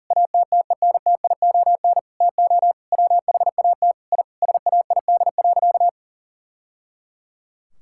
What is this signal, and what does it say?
Morse code, heard as a keyed tone, about 40 wpm, 700 Hz. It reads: ATTENTION TO WHAT I SAID.
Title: Sentences from Top 1000 Words - Review Part 6 - 40wpm